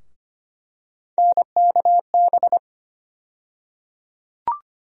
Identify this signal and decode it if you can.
Morse code, heard as a keyed tone, about 25 wpm, 700 Hz. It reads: NKB
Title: USA Call Sign Postfixes - 3 Letters 25wpm